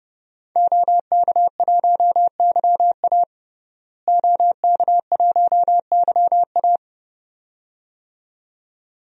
Morse code, heard as a keyed tone, about 30 wpm, 700 Hz. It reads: OK1YA OK1YA
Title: DXCC-100 Call Signs - 5 Characters - Repeated 2x - 30wpm